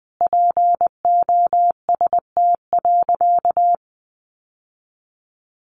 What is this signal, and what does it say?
Morse code, heard as a keyed tone, about 20 wpm, 700 Hz. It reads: POST.
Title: Sentences from Top 1000 Words 20wpm